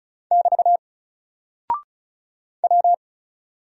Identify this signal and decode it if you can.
Morse code, heard as a keyed tone, about 36 wpm, 700 Hz. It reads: <BT> W
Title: Single Character (Letter:Number:Punctuation:Prosign) Alt 35wpm